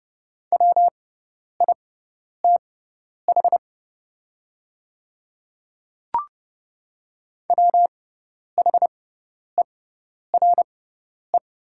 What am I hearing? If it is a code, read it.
Morse code, heard as a keyed tone, about 32 wpm, spaced as for 10 wpm, 700 Hz. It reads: WITH WHERE
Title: 6x Character Spacing - Top 300 Words - 30wpm